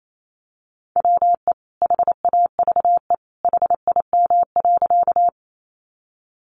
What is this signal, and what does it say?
Morse code, heard as a keyed tone, about 28 wpm, 700 Hz. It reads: WE HAVE HIM.
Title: Sentences from Top 100 Words 28wpm